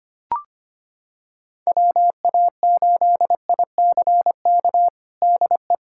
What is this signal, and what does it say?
Morse code, heard as a keyed tone, about 25 wpm, 700 Hz. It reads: WA8ICK DE
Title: USA Callsign DE Callsign 25wpm